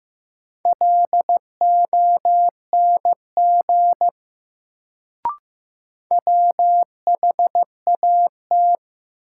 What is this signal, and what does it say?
Morse code, heard as a keyed tone, about 15 wpm, 700 Hz. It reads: LONG WHAT
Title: Sentences from Top 100 Words - New Word Drill - 15wpm